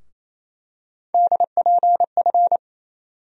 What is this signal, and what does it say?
Morse code, heard as a keyed tone, about 28 wpm, 700 Hz. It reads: DPF
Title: USA Call Sign Postfixes - 3 Letters 28wpm